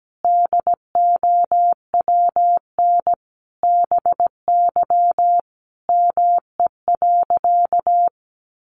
Morse code, heard as a keyed tone, about 17 wpm, 700 Hz. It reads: DOWN BY ME.
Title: Sentences from Top 500 Words 17wpm